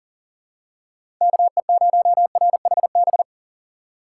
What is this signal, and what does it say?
Morse code, heard as a keyed tone, about 40 wpm, 700 Hz. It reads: KE0RHB
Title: USA Call Signs - 6 Characters - 3x Character Spacing - 40wpm